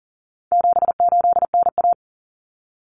Morse code, heard as a keyed tone, about 40 wpm, 700 Hz. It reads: 78NA
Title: Common Cut Numbers - 4-Digit 40wpm